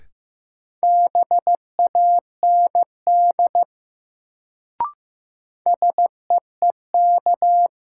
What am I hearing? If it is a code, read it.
Morse code, heard as a keyed tone, about 15 wpm, 700 Hz. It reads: BAND SEEK
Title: Sentences from Top 1600 Words - New Word Drill - 15wpm